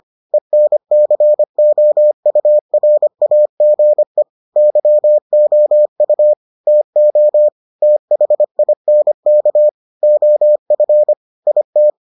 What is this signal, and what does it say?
Morse code, heard as a keyed tone, about 25 wpm, 600 Hz. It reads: ENCOURAGE YOU TO THINK OF IT